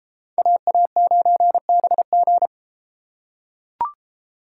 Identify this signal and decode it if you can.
Morse code, heard as a keyed tone, about 29 wpm, 700 Hz. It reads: AA9BG